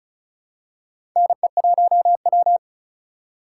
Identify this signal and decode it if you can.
Morse code, heard as a keyed tone, about 34 wpm, 700 Hz. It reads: NE1W
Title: USA Call Signs - 4 Characters - 3x Character Spacing - 35wpm